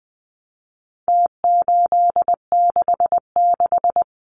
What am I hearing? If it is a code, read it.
Morse code, heard as a keyed tone, about 20 wpm, 700 Hz. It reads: T866
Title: Numbers - Contest Style - 4-Digit 20wpm